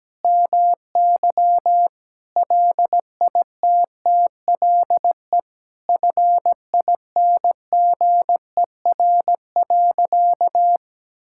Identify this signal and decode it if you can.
Morse code, heard as a keyed tone, about 17 wpm, 700 Hz. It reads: MY LITTLE FINGER.
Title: Sentences from Top 1600 Words 17wpm